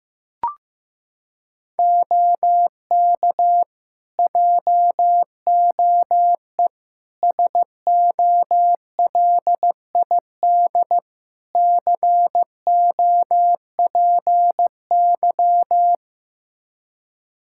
Morse code, heard as a keyed tone, about 15 wpm, 700 Hz. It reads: OK JOE SOLID COPY
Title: Practice QSO - Exchange 3 - More Info - 15wpm